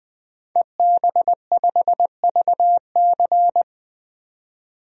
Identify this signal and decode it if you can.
Morse code, heard as a keyed tone, about 20 wpm, 700 Hz. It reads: EB5VC
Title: DXCC-100 Call Signs - 5 Characters - 20wpm